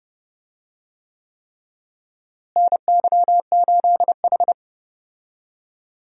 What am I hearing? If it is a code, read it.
Morse code, heard as a keyed tone, about 31 wpm, 700 Hz. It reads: NY8H
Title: USA Call Signs - 4 Characters - Repeated 3x - 30wpm